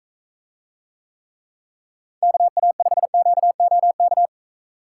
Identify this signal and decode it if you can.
Morse code, heard as a keyed tone, about 43 wpm, 700 Hz. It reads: KA5QOK